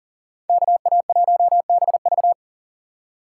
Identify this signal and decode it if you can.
Morse code, heard as a keyed tone, about 40 wpm, 700 Hz. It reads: KA1BV
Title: USA Call Signs - 5 Characters - 3x Character Spacing - 40wpm